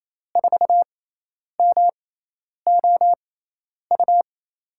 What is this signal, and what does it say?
Morse code, heard as a keyed tone, about 28 wpm, spaced as for 10 wpm, 700 Hz. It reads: 4MOU
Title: USA Call Signs - 6 Characters - 6x Character Spacing - 28wpm